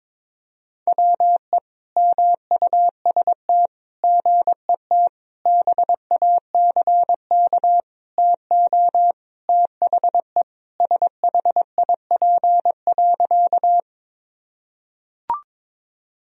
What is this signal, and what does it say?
Morse code, heard as a keyed tone, about 22 wpm, 700 Hz. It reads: WE MUST GET BACK TO THE SHIP.
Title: Sentences from Top 1000 Words - Review Part 2 - 22wpm